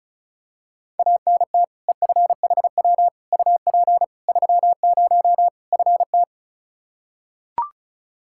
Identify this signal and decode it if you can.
Morse code, heard as a keyed tone, about 36 wpm, 700 Hz. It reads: ANT EFHW UP 30 FT
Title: Practice QSO - Exchange 2 - Info - 35wpm